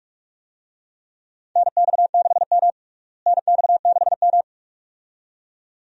Morse code, heard as a keyed tone, about 47 wpm, 700 Hz. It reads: NX6M NX6M
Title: USA Call Signs - 4 Characters - Repeated 2x - 45wpm